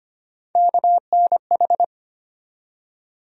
Morse code, heard as a keyed tone, about 25 wpm, 700 Hz. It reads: KNH
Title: USA Call Sign Postfixes - 3 Letters 25wpm